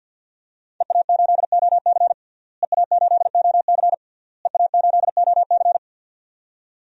Morse code, heard as a keyed tone, about 46 wpm, 700 Hz. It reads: EA8OC EA8OC EA8OC
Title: DXCC-100 Call Signs - 5 Characters - Repeated 3x - 50wpm